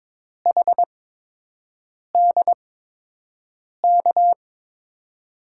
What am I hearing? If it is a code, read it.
Morse code, heard as a keyed tone, about 22 wpm, spaced as for 6 wpm, 700 Hz. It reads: HDK
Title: USA Call Signs - 8x Character Spacing - 22wpm